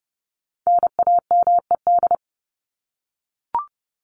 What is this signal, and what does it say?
Morse code, heard as a keyed tone, about 30 wpm, 700 Hz. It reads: NAMED
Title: ED Suffix - Top 100 Words with Base Words 30wpm